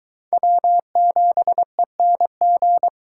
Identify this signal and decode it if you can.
Morse code, heard as a keyed tone, about 23 wpm, 700 Hz. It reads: W7ENG